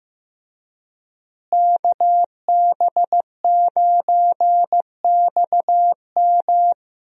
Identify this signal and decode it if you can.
Morse code, heard as a keyed tone, about 16 wpm, 700 Hz. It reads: KB9XM
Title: USA Call Signs - 5 Characters - Repeated 3x - 15wpm